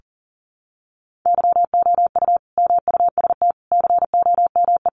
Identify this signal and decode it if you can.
Morse code, heard as a keyed tone, about 40 wpm, 700 Hz. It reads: YOU MUST COME
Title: Sentences from Top 200 Words 40wpm